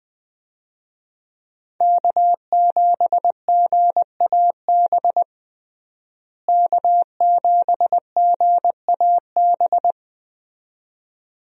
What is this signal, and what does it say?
Morse code, heard as a keyed tone, about 20 wpm, 700 Hz. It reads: K7GAB K7GAB
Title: USA Call Signs - 5 Characters - Repeated 2x - 20wpm